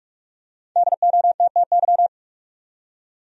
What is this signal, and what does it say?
Morse code, heard as a keyed tone, about 46 wpm, 700 Hz. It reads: DOTTY
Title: USA Common Names - 45wpm